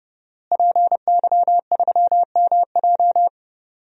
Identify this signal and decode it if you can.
Morse code, heard as a keyed tone, about 31 wpm, 700 Hz. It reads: PY3MJ
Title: DXCC-100 Call Signs - 5 Characters - Repeated 2x - 30wpm